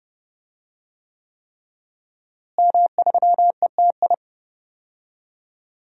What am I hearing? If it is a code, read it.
Morse code, heard as a keyed tone, about 30 wpm, 700 Hz. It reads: M3ETI
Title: DXCC-100 Call Signs - 5 Characters - Repeated 3x - 30wpm